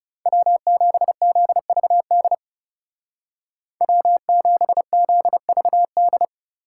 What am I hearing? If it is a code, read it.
Morse code, heard as a keyed tone, about 36 wpm, 700 Hz. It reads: W7ZVD W7ZVD
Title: Speed Racing - USA 1x3 Call Signs 30wpm